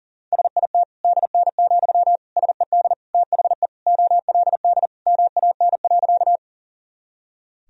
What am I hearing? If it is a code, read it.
Morse code, heard as a keyed tone, about 37 wpm, 700 Hz. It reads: SIT DN, SED THE OLD MAN.
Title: Sentences from Top 500 Words and Abbreviations 40wpm